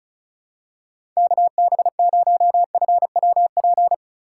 Morse code, heard as a keyed tone, about 36 wpm, 700 Hz. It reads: KB0FWP
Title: USA Call Signs - 6 Characters - 3x Character Spacing - 35wpm